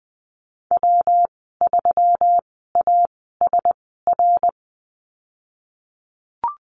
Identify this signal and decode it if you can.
Morse code, heard as a keyed tone, about 20 wpm, spaced as for 15 wpm, 700 Hz. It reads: W3ASR
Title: 173 Lesson 28g - Number 9 - Call Signs 20wpm